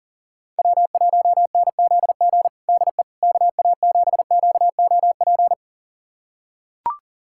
Morse code, heard as a keyed tone, about 40 wpm, 700 Hz. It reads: W1NZG DE KA7QOP
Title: USA Callsign DE Callsign 40wpm